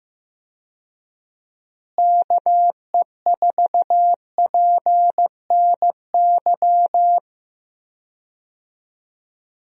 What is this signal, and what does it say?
Morse code, heard as a keyed tone, about 15 wpm, 700 Hz. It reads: KE4PNY
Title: USA Call Signs - 6 Characters - Repeated 3x - 15wpm